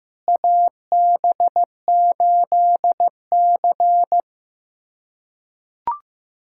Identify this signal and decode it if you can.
Morse code, heard as a keyed tone, about 15 wpm, 700 Hz. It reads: AB8C
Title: USA Callsign DE Callsign - 4 Characters - 15wpm